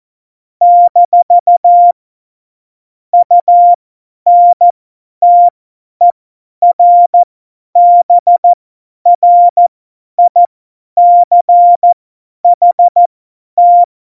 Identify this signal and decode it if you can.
Morse code, heard as a keyed tone, about 14 wpm, spaced as for 10 wpm, 700 Hz. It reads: - UNTERBRICHT
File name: Morsekurs / Witze / Witz 19. Januar 2021, 50 F70